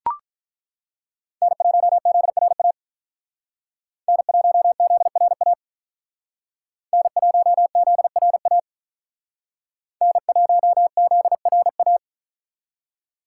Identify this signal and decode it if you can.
Morse code, heard as a keyed tone, about 41 wpm, 700 Hz. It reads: N1ZRA N1ZRA N1ZRA N1ZRA
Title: Speed Racing - USA 1x3 Call Signs 35wpm